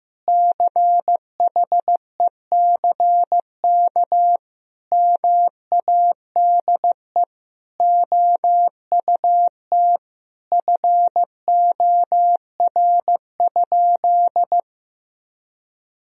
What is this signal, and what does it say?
Morse code, heard as a keyed tone, about 15 wpm, 700 Hz. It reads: CHECK MADE OUT FOR?
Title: Sentences from Top 1000 Words - Review Part 3 - 15wpm